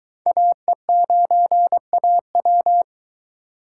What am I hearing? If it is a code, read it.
Morse code, heard as a keyed tone, about 23 wpm, 700 Hz. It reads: AE9AW